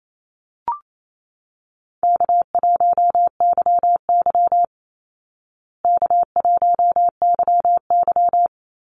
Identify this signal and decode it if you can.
Morse code, heard as a keyed tone, about 28 wpm, 700 Hz. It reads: K1YY K1YY
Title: Easy Call Signs Repeated 3x - 28wpm